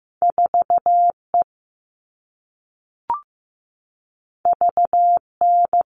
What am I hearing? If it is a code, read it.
Morse code, heard as a keyed tone, about 15 wpm, 700 Hz. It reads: 4E VN
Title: All Cut Numbers - 2-Digit 15wpm